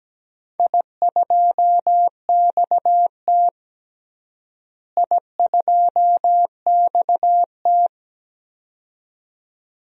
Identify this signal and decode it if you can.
Morse code, heard as a keyed tone, about 17 wpm, 700 Hz. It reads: I2XT I2XT
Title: DXCC-100 Call Signs - 4 Characters - Repeated 2x - 17wpm